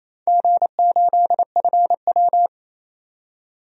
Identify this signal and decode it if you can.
Morse code, heard as a keyed tone, about 28 wpm, 700 Hz. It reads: G8FW
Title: DXCC-100 Call Signs - 4 Characters - 28wpm